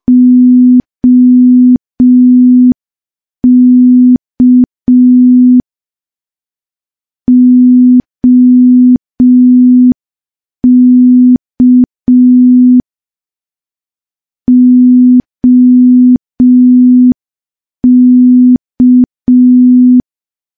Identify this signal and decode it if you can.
Morse code, heard as a keyed tone, about 5 wpm, 250 Hz. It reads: OK OK OK
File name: morse / morse-training-05WPM